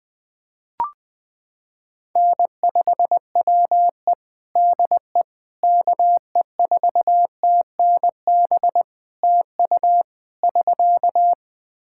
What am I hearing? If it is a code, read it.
Morse code, heard as a keyed tone, about 20 wpm, 700 Hz. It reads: N5WE DE KE4TNB TU <SK>
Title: K1USN SST QSO Practice 20wpm